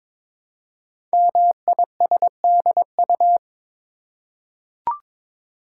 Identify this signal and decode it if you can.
Morse code, heard as a keyed tone, about 22 wpm, 700 Hz. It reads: MISDU
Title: Code Groups - 5 Letters - 22wpm